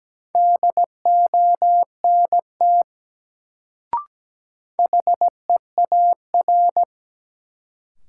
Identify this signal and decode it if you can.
Morse code, heard as a keyed tone, about 17 wpm, 700 Hz. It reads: DONT HEAR
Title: Sentences from Top 300 Words - New Word Drill - 17wpm